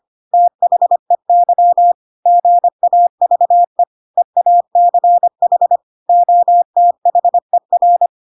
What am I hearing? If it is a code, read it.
Morse code, heard as a keyed tone, about 25 wpm, 700 Hz. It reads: THEY GAVE EACH OTHER
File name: Pinocchio / Pinocchio ch2 25 25 ik1hge